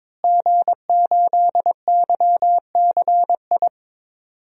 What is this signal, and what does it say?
Morse code, heard as a keyed tone, about 22 wpm, 700 Hz. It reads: G8YCI
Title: DXCC-100 Call Signs - 5 Characters - Repeated 2x - 22wpm